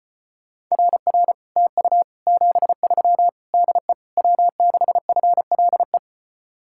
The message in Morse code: RR TU 73 DE W6FLE

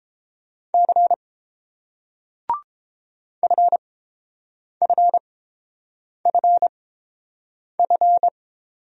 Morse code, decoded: C FFFF

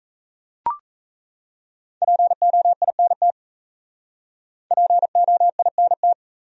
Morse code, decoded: POINT POINT

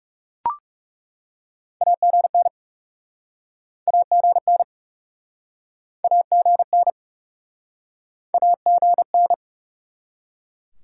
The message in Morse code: AGN AGN AGN AGN

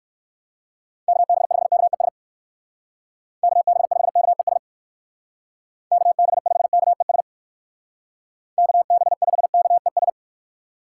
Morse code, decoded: KB5KES KB5KES KB5KES KB5KES

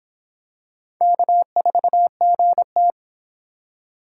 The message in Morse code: K4GT

